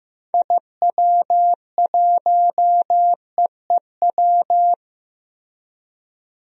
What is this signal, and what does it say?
Morse code, heard as a keyed tone, about 15 wpm, 700 Hz. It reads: IW1EEW
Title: DXCC-100 Call Signs - 6 Characters - 15wpm